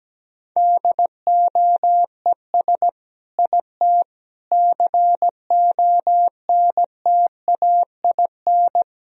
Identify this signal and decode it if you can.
Morse code, heard as a keyed tone, about 17 wpm, 700 Hz. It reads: DOES IT CONTAIN
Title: Sentences from Top 1000 Words - Review Part 2 - 17wpm